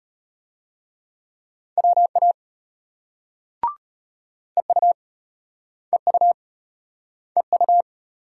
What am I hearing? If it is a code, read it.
Morse code, heard as a keyed tone, about 34 wpm, 700 Hz. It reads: WA EU EU EU